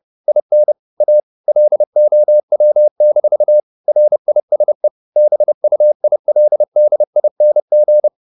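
Morse code, heard as a keyed tone, about 30 wpm, 600 Hz. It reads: IN A LOW-RISE BUILDING
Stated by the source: Morse code at 29 wpm, sending Ken Follett's novel Never in English, from CW med Gnister.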